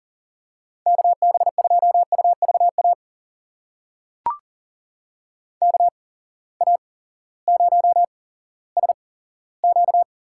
Morse code, heard as a keyed tone, about 40 wpm, spaced as for 18 wpm, 700 Hz. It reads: KB2UVA KA0SQ
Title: USA Call Signs - 6 Characters - 8x Character Spacing - 40wpm